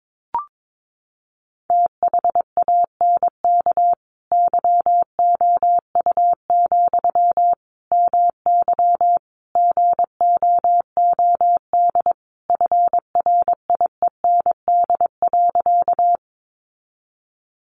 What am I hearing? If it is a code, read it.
Morse code, heard as a keyed tone, about 22 wpm, 700 Hz. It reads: THANK YOU, MY GOOD FRIEND.
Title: Sentences from Top 600 Words 22wpm